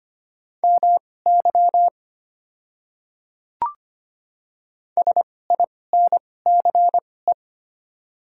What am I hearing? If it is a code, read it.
Morse code, heard as a keyed tone, about 25 wpm, spaced as for 18 wpm, 700 Hz. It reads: MY SINCE
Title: 2x Character Spacing - Top 300 Words - 25wpm